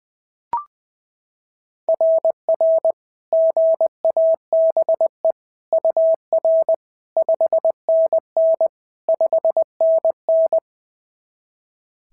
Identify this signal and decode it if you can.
Morse code, heard as a keyed tone, about 18 wpm, 650 Hz. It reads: RR GABE UR 5NN 5NN
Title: SOTA QSO Practice 20wpm